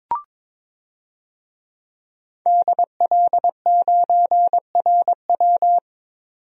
Morse code, heard as a keyed tone, about 21 wpm, 700 Hz. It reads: DL9RW
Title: DXCC-100 Call Signs - 5 Characters - 22wpm